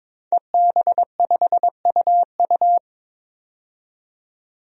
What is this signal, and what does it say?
Morse code, heard as a keyed tone, about 22 wpm, 700 Hz. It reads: EB5UU
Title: DXCC-100 Call Signs - 5 Characters - 22wpm